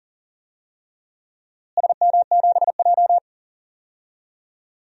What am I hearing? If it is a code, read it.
Morse code, heard as a keyed tone, about 40 wpm, 700 Hz. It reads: SM7J
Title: DXCC-100 Call Signs - 4 Characters - Repeated 3x - 40wpm